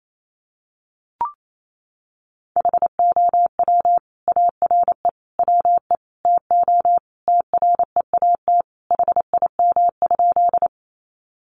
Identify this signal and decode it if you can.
Morse code, heard as a keyed tone, about 28 wpm, 700 Hz. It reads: HOW ARE WE TO TREAT HIM?
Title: Sentences from Top 1300 Words 28wpm